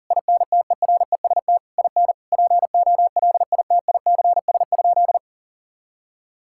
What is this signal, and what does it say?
Morse code, heard as a keyed tone, about 40 wpm, 700 Hz. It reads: INTEREST IN POLITICS?
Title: Sentences from Top 1000 Words - Review Part 6 - 40wpm